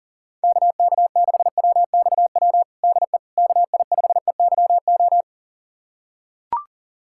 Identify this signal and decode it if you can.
Morse code, heard as a keyed tone, about 37 wpm, 700 Hz. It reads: KK6WXW DE KI5EYO
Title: USA Callsign DE Callsign - 6 Characters - 40wpm